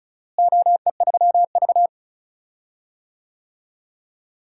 Morse code, heard as a keyed tone, about 36 wpm, 700 Hz. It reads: OE3V
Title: DXCC-100 Call Signs - 4 Characters - Repeated 3x - 35wpm